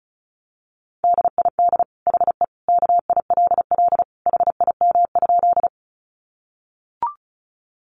Morse code, heard as a keyed tone, about 34 wpm, 700 Hz. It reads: DID HE KILL HIM?
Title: Sentences from Top 1400 Words 35wpm